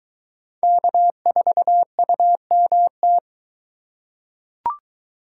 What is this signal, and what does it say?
Morse code, heard as a keyed tone, about 22 wpm, 700 Hz. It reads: K4UMT